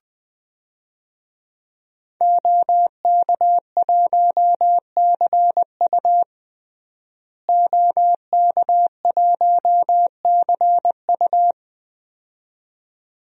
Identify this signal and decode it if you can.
Morse code, heard as a keyed tone, about 20 wpm, 700 Hz. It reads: OK1CU OK1CU
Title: DXCC-100 Call Signs - 5 Characters - Repeated 2x - 20wpm